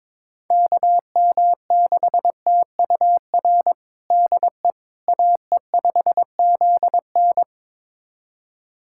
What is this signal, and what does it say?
Morse code, heard as a keyed tone, about 22 wpm, 700 Hz. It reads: KM6TUR DE AE5ZN
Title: USA Callsign DE Callsign 22wpm